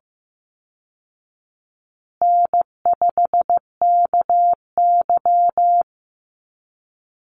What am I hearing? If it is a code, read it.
Morse code, heard as a keyed tone, about 15 wpm, 700 Hz. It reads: N5KY
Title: Easy Call Signs Repeated 3x - 15wpm